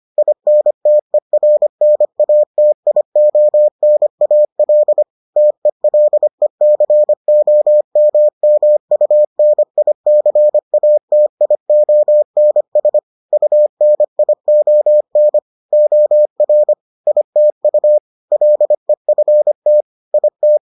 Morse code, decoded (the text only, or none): INTERNATIONAL TELECOMMUNICATIONS UNION OR ITU LEFT IT